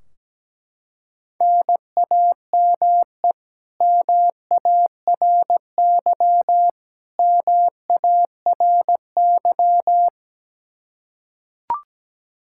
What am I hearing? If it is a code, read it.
Morse code, heard as a keyed tone, about 17 wpm, 700 Hz. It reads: NAME MARY MARY
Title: K1USN SST QSO Practice 17wpm